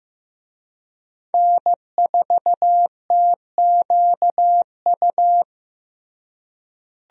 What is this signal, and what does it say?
Morse code, heard as a keyed tone, about 15 wpm, 700 Hz. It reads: N4TQU
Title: USA Call Signs - 5 Characters - Repeated 2x - 15wpm